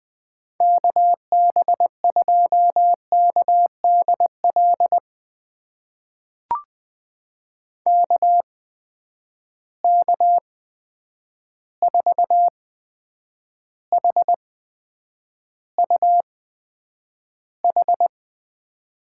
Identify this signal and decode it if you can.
Morse code, heard as a keyed tone, about 20 wpm, spaced as for 9 wpm, 700 Hz. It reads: KB2KDL KK4HUH
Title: USA Call Signs - 6 Characters - 8x Character Spacing - 20wpm